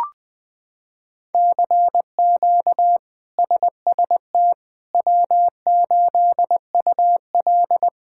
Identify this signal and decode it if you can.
Morse code, heard as a keyed tone, about 19 wpm, 700 Hz. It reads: CQ SST W8UL
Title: K1USN SST QSO Practice - 20wpm